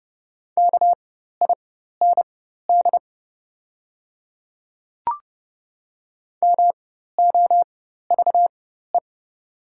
Morse code, heard as a keyed tone, about 30 wpm, spaced as for 14 wpm, 700 Hz. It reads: KIND MOVE
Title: 4x Character Spacing - Top 200 Words - 30wpm